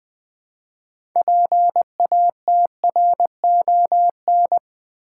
Morse code, heard as a keyed tone, about 20 wpm, 700 Hz. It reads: PATRON
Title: Build A Word - 3 Letters At A Time - 6-Letter Words - 20wpm